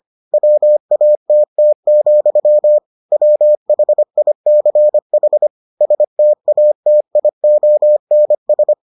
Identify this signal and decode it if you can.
Morse code, heard as a keyed tone, about 25 wpm, 600 Hz. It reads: WATT, WHICH STATIONS